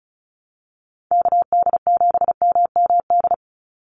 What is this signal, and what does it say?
Morse code, heard as a keyed tone, about 34 wpm, 700 Hz. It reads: KD7MMD
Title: USA Call Signs 35wpm